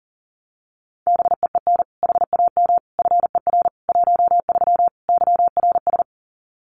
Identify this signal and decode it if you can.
Morse code, heard as a keyed tone, about 40 wpm, 700 Hz. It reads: BEEN HAM FER 13 YRS
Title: Common QSO Phrases 40wpm